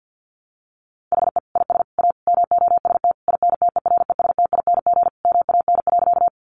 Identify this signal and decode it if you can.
Morse code, heard as a keyed tone, about 50 wpm, 700 Hz. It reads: HE IS A MOST INTERESTING MAN.
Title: Sentences from Top 1000 Words 50wpm